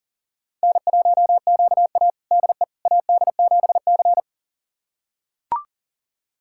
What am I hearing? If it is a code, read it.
Morse code, heard as a keyed tone, about 40 wpm, 700 Hz. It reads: N1QA DE AD7C
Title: USA Callsign DE Callsign - 4 Characters - 40wpm